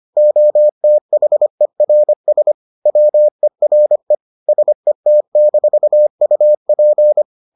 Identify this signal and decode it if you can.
Morse code, heard as a keyed tone, about 25 wpm, 600 Hz. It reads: OTHERS WERE SET-UP